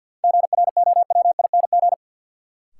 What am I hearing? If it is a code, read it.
Morse code, heard as a keyed tone, about 49 wpm, 700 Hz. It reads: GROWING
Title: Sets of 5 Words Encore - New Word Drill - 50wpm